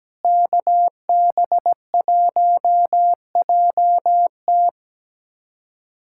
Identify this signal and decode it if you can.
Morse code, heard as a keyed tone, about 17 wpm, 700 Hz. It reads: KB1JT
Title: USA Callsign DE Callsign - 5 Characters - 17wpm